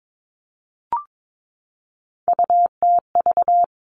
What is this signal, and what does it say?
Morse code, heard as a keyed tone, about 22 wpm, 700 Hz. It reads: UTV
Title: Three Challenging Letter-Number Combinations 22wpm